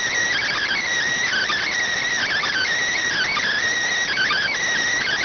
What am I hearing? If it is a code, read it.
Emergency Alert System (SAME) tones (header bursts), heard as harsh buzzing bursts. It is heard from 0.3 to 4.6 s.